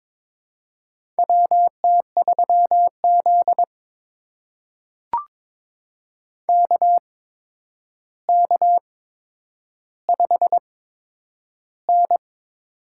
Morse code, heard as a keyed tone, about 22 wpm, spaced as for 9 wpm, 700 Hz. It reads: WT3Z KK5N
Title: USA Call Signs - 4 Characters - 8x Character Spacing - 22wpm